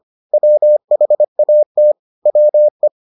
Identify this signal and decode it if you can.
Morse code, heard as a keyed tone, about 25 wpm, 600 Hz. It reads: WHAT WE